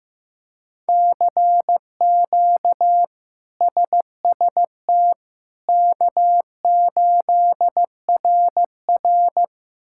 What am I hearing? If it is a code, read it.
Morse code, heard as a keyed tone, about 15 wpm, 700 Hz. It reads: CQ SST K8RR
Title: K1USN SST QSO Practice - 15wpm